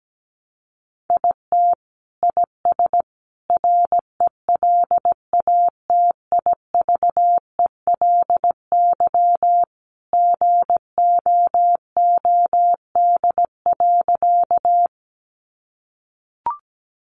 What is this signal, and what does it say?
Morse code, heard as a keyed tone, about 17 wpm, 700 Hz. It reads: IT IS RELATIVELY GOOD.
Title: Sentences from Top 1100 Words 17wpm